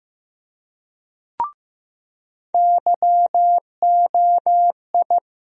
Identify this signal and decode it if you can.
Morse code, heard as a keyed tone, about 16 wpm, 700 Hz. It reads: YOI
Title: USA Call Sign Postfixes - 3 Letters 15wpm